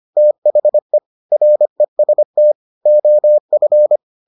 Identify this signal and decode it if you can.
Morse code, heard as a keyed tone, about 25 wpm, 600 Hz. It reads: THE REST OF